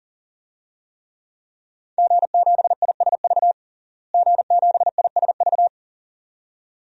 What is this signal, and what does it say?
Morse code, heard as a keyed tone, about 40 wpm, 700 Hz. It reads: G7ISV G7ISV
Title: DXCC-100 Call Signs - 5 Characters - Repeated 2x - 40wpm